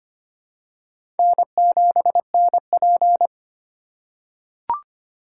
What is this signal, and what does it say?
Morse code, heard as a keyed tone, about 25 wpm, 700 Hz. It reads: N7NP
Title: USA Call Signs - 4 Characters - 3x Character Spacing - 25wpm